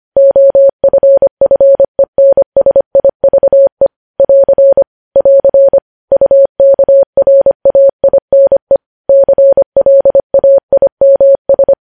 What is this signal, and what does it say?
Morse code, heard as a keyed tone, about 25 wpm, 550 Hz. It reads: OFFENSIVE <AR> <AR> UKRAINE CLAIMS
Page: News Headlines in Morse Code at 25 WPM